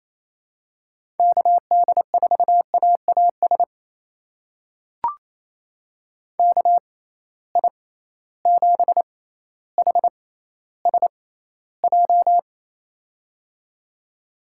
Morse code, heard as a keyed tone, about 28 wpm, spaced as for 14 wpm, 700 Hz. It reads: KD4AAS KI7HSJ
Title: USA Call Signs - 6 Characters - 6x Character Spacing - 28wpm